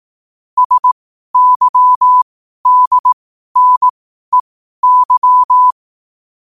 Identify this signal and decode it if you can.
Morse code, heard as a keyed tone, about 18 wpm, spaced as for 12 wpm, 1000 Hz. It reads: SYDNEY
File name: qotdfiles / 12wpm